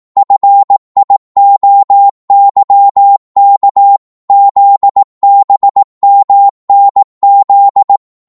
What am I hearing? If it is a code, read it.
Morse code, heard as a keyed tone, about 18 wpm, 800 Hz. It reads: FIOYK ZBMNZ